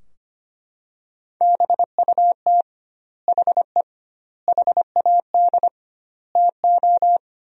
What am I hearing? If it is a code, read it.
Morse code, heard as a keyed tone, about 25 wpm, 700 Hz. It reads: BUT HE HAD TO
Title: 2x Word-Spacing - Sentences from Top 100 Words - 25wpm